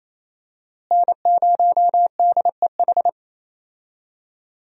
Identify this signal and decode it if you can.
Morse code, heard as a keyed tone, about 27 wpm, 700 Hz. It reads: N0DEH
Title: USA Call Signs - 5 Characters - Repeated 3x - 28wpm